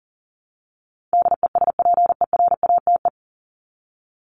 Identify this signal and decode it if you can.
Morse code, heard as a keyed tone, about 40 wpm, 700 Hz. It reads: DESPERATE
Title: Top 1751-2000 Words 40wpm